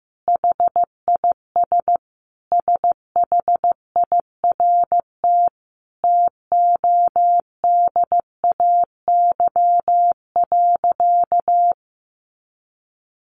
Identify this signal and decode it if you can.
Morse code, heard as a keyed tone, about 15 wpm, 700 Hz. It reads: HIS SHIRT TODAY.
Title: Additional Easy Phrases 15wpm